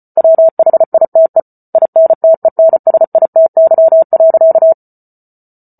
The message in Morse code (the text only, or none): WHITE INTENSITY.